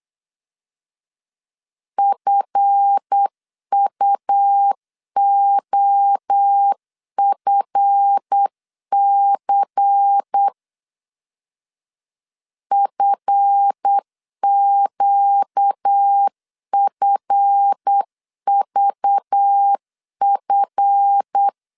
Morse code, heard as a keyed tone, about 8 wpm, 800 Hz. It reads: FUOFC FQFVF
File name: LEZIONI / 7 GF9Z / 7 GF 8wpm 3108